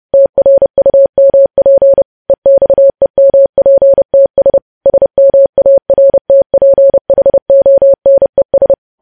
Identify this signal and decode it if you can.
Morse code, heard as a keyed tone, about 30 wpm, 550 Hz. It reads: TRUMP EXEMPTS SMARTPHONES
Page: news headlines in morse code at 30 wpm